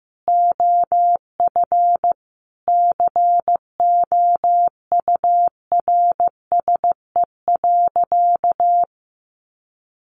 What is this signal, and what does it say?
Morse code, heard as a keyed tone, about 15 wpm, 700 Hz. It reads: OF COURSE.
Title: Sentences from Top 900 Words 15wpm